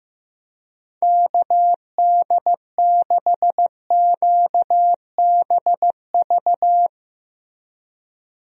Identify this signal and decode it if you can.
Morse code, heard as a keyed tone, about 15 wpm, 700 Hz. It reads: KD6QBV
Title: USA Call Signs - 6 Characters - Repeated 2x - 15wpm